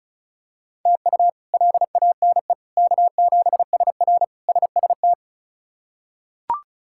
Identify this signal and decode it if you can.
Morse code, heard as a keyed tone, about 36 wpm, 700 Hz. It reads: TU LANE K7SR SST
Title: K1USN SST QSO Practice - 35wpm